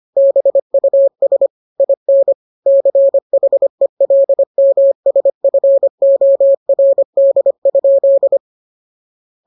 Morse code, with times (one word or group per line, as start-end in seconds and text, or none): 0.2-1.5 BUS
1.8-2.3 IN
2.7-8.4 CHELMSFORD?